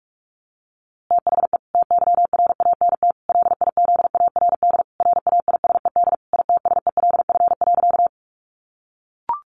Morse code, heard as a keyed tone, about 44 wpm, 700 Hz. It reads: THE TYRANT LIZARD RAISED ITSELF.
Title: A Sound Of Thunder 45wpm